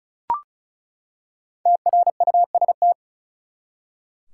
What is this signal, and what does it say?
Morse code, heard as a keyed tone, about 36 wpm, 700 Hz. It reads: TRUST
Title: Five Letter Words - 35wpm